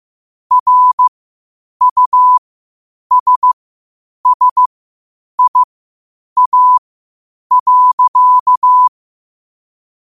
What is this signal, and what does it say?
Morse code, heard as a keyed tone, about 15 wpm, spaced as for 8 wpm, 1000 Hz. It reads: RUSSIA.